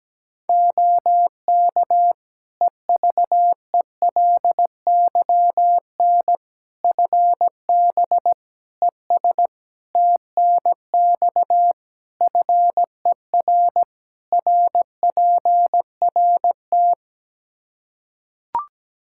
Morse code, heard as a keyed tone, about 17 wpm, 700 Hz. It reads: OK EVELYN FB ES TNX FER RPRT
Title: Practice QSO - Exchange 2 - Info - 17wpm